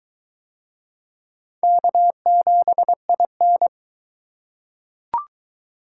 Morse code, heard as a keyed tone, about 20 wpm, 700 Hz. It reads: K7IN